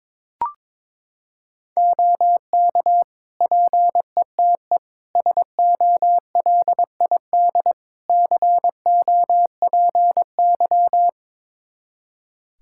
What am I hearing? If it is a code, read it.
Morse code, heard as a keyed tone, about 22 wpm, 700 Hz. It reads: OK PETE SOLID COPY
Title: Practice QSO - Exchange 3 - More Info - 22wpm